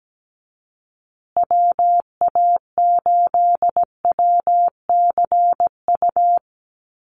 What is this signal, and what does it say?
Morse code, heard as a keyed tone, about 17 wpm, 700 Hz. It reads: WA8WCU
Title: USA Call Signs Repeated 3x 17wpm